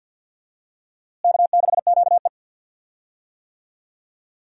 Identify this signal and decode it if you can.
Morse code, heard as a keyed tone, about 48 wpm, 700 Hz. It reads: K6QE
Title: USA Call Signs - 4 Characters - Repeated 3x - 50wpm